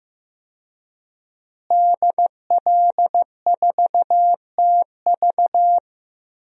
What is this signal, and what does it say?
Morse code, heard as a keyed tone, about 15 wpm, 700 Hz. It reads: DL4TV
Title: DXCC-100 Call Signs - 5 Characters - Repeated 3x - 15wpm